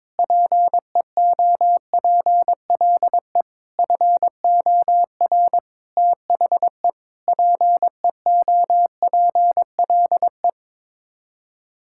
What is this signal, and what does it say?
Morse code, heard as a keyed tone, about 21 wpm, 700 Hz. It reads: PEOPLE FOR THE PEOPLE
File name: Sets of 9 Words - 22wpm